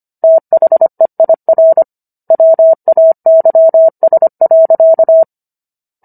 Morse code, heard as a keyed tone, about 25 wpm, 650 Hz. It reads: THEIR WAYS.